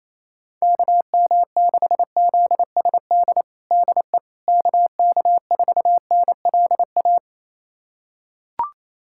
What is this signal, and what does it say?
Morse code, heard as a keyed tone, about 28 wpm, 700 Hz. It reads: KM6ZSD DE KK4NLA
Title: USA Callsign DE Callsign - 6 Characters - 28wpm